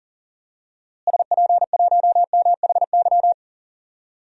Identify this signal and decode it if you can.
Morse code, heard as a keyed tone, about 40 wpm, 700 Hz. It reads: SP1MHY
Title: DXCC-100 Call Signs - 6 Characters - Repeated 3x - 40wpm